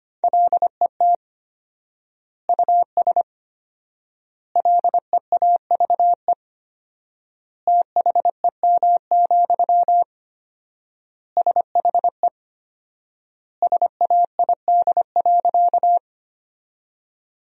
Morse code, heard as a keyed tone, about 25 wpm, 700 Hz. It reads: LET US LEAVE THEM, SHE SAID.
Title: 4x Word-Spacing - Sentences from Top 400 Words - 25wpm